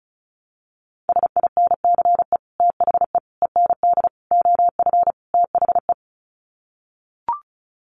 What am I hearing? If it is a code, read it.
Morse code, heard as a keyed tone, about 34 wpm, 700 Hz. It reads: SINCE THE END OF THE
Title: Sets of 5 Words Encore 35wpm